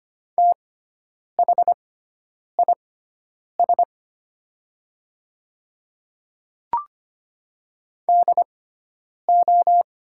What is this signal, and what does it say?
Morse code, heard as a keyed tone, about 25 wpm, spaced as for 9 wpm, 700 Hz. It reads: THIS DO
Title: 6x Character Spacing - Top 100 Words - 25wpm